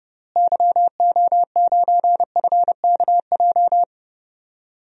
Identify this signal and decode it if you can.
Morse code, heard as a keyed tone, about 30 wpm, 700 Hz. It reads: YO9FKJ
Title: DXCC-100 Call Signs - 6 Characters - Repeated 2x - 30wpm